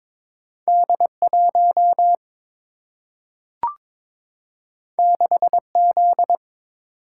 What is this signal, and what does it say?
Morse code, heard as a keyed tone, about 22 wpm, 700 Hz. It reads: D1 6Z
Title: Single-Letter Single-Number 22wpm